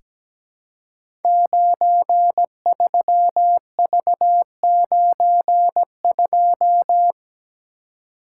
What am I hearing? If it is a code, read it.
Morse code, heard as a keyed tone, about 17 wpm, 700 Hz. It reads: 93V92
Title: Code Groups - 5 Letter:Numbers - 17wpm